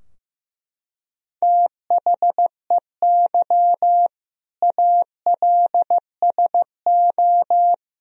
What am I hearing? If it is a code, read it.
Morse code, heard as a keyed tone, about 15 wpm, 700 Hz. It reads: THEY ALSO
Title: Sentences from Top 1000 Words - Review Part 2 - 15wpm